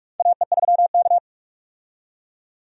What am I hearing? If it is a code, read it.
Morse code, heard as a keyed tone, about 46 wpm, 700 Hz. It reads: AE3K
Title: USA Call Signs - 4 Characters - 45wpm